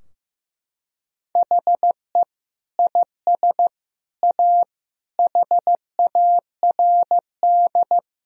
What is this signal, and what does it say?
Morse code, heard as a keyed tone, about 15 wpm, 700 Hz. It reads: HE IS A HARD
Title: Sentences from Top 1000 Words - Review Part 4 - 15wpm